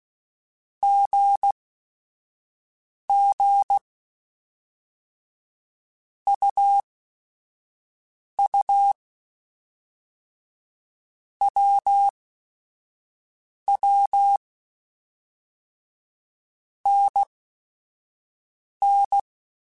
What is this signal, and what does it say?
Morse code, heard as a keyed tone, about 16 wpm, spaced as for 5 wpm, 800 Hz. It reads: GG UU WW NN